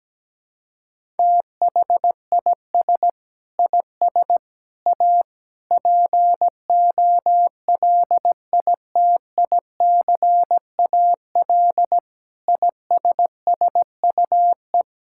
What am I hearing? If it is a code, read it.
Morse code, heard as a keyed tone, about 17 wpm, 700 Hz. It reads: THIS IS A POLITICAL ISSUE
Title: Sentences from Top 1000 Words - Review Part 6 - 17wpm